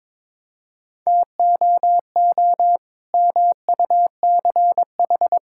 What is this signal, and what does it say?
Morse code, heard as a keyed tone, about 21 wpm, 700 Hz. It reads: TOO MUCH
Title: Sentences from Top 1000 Words - Review Part 5 - 22wpm